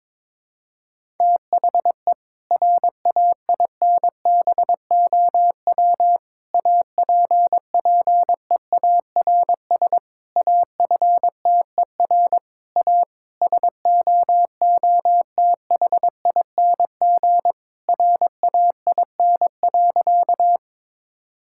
Morse code, heard as a keyed tone, about 22 wpm, 700 Hz. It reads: THE RAINBOW APPEARS AFTER A SOOTHING RAIN.
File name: Difficult Sentences with Hints - 22wpm